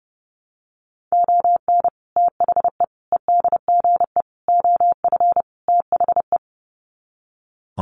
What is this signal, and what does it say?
Morse code, heard as a keyed tone, about 30 wpm, 700 Hz. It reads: ON THE EDGE OF THE
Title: Sets of 5 Words 30wpm